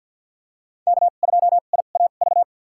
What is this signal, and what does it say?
Morse code, heard as a keyed tone, about 51 wpm, spaced as for 36 wpm, 700 Hz. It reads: K2IAV
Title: USA Call Signs - 5 Characters - 2x Character Spacing - 50wpm